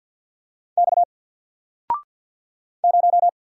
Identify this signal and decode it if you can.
Morse code, heard as a keyed tone, about 50 wpm, 700 Hz. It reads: X 0
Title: Single Letter-Number 50wpm